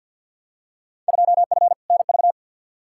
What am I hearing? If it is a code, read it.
Morse code, heard as a keyed tone, about 47 wpm, 700 Hz. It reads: 2F NV
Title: Field Day Exchanges - 50wpm